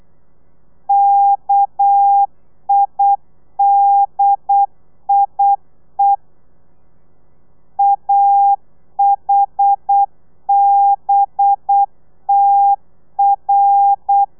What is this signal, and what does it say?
Morse code, heard as a keyed tone, about 8 wpm, 800 Hz. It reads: KIDIE AHBTR
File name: telegrafia / i2tfj / Lez11 35